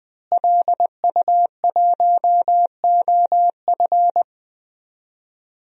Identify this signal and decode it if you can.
Morse code, heard as a keyed tone, about 20 wpm, 700 Hz. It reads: LU1OF
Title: DXCC-100 Call Signs - 5 Characters - Repeated 3x - 20wpm